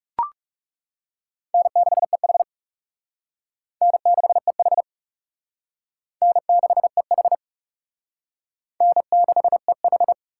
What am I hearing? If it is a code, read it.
Morse code, heard as a keyed tone, about 38 wpm, 700 Hz. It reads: N6EH N6EH N6EH N6EH